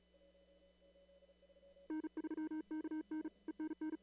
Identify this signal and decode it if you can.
Morse code, heard as a keyed tone, about 36 wpm, 310 Hz. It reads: N3KN ENN
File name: CW / kt5j wpx cw-op-k5pi